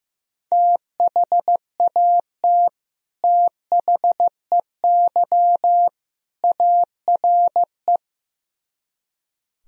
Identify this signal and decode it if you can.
Morse code, heard as a keyed tone, about 15 wpm, 700 Hz. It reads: THAT THEY ARE
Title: Speed-Racing - Sets of 3 Words 15wpm